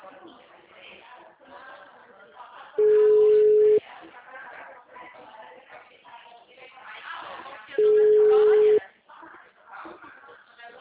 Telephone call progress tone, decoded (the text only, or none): ringback tone